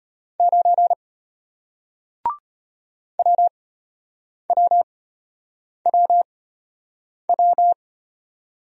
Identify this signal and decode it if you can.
Morse code, heard as a keyed tone, about 34 wpm, spaced as for 8 wpm, 700 Hz. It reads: 9 WWWW